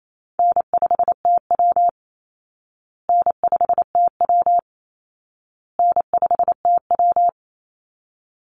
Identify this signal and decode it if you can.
Morse code, heard as a keyed tone, about 28 wpm, 700 Hz. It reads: N5TW N5TW N5TW
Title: Easy Call Signs Repeated 3x - 28wpm